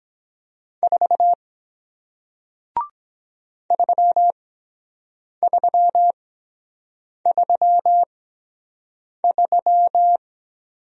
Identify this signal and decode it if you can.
Morse code, heard as a keyed tone, about 23 wpm, spaced as for 7 wpm, 700 Hz. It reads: 4 3333